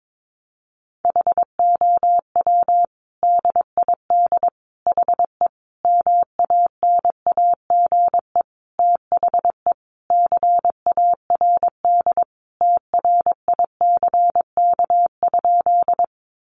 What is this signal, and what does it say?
Morse code, heard as a keyed tone, about 22 wpm, 700 Hz. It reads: HOW DID HE MANAGE THE CARD TRICK?
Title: Sentences from Top 1500 Words 22wpm